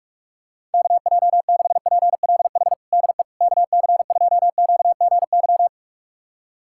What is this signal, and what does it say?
Morse code, heard as a keyed tone, about 46 wpm, 700 Hz. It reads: KJ6PLH DE KC2QGY
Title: USA Callsign DE Callsign - 6 Characters - 45wpm